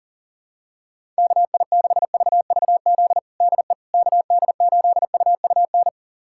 Morse code, heard as a keyed tone, about 40 wpm, 700 Hz. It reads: KI6VVZ DE KD8UUN
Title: USA Callsign DE Callsign - 6 Characters - 40wpm